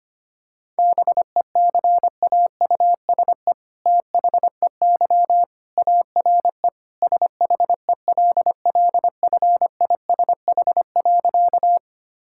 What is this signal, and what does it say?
Morse code, heard as a keyed tone, about 25 wpm, 700 Hz. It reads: BECAUSE THEY ARE SHELLFISH.
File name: practice / 25wpm / All-Age Appropriate Jokes - 25wpm